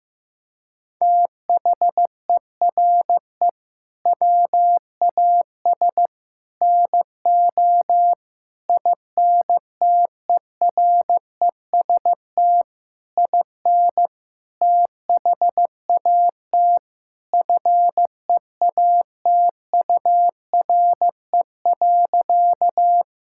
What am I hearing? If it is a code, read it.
Morse code, heard as a keyed tone, about 15 wpm, 700 Hz. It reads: THERE WAS NO INTEREST IN THAT FEATURE.
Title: Sentences from Top 1000 Words - Review Part 5 - 15wpm